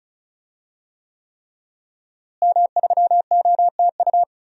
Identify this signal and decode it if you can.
Morse code, heard as a keyed tone, about 34 wpm, 700 Hz. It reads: M3OTU
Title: DXCC-100 Call Signs - 5 Characters - Repeated 3x - 35wpm